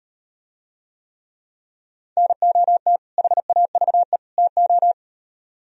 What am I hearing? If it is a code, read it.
Morse code, heard as a keyed tone, about 38 wpm, 700 Hz. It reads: NOT HAVE TO